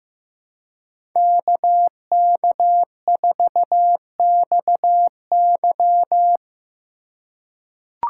Morse code, heard as a keyed tone, about 15 wpm, 700 Hz. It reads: KK4XY
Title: USA Call Signs - 2x Character Spacing - 15wpm